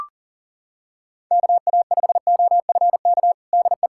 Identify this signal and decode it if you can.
Morse code, heard as a keyed tone, about 37 wpm, 700 Hz. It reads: KA5OFK DE
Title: USA Callsign DE Callsign - 6 Characters - 40wpm